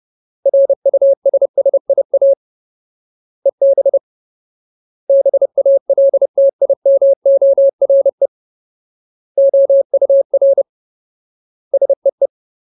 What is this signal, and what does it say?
Morse code, heard as a keyed tone, about 30 wpm, 550 Hz. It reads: RUSSIA EB BALTIMORE OUR SEE
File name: Ultimate Word List / 30WPM Ultimate